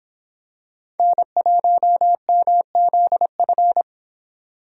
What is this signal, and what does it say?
Morse code, heard as a keyed tone, about 26 wpm, 700 Hz. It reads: N1MZF